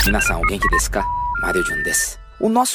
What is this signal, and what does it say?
Special information tone, with speech.